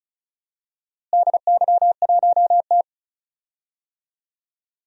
Morse code, heard as a keyed tone, about 36 wpm, 700 Hz. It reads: DY1T
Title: DXCC-100 Call Signs - 4 Characters - Repeated 3x - 35wpm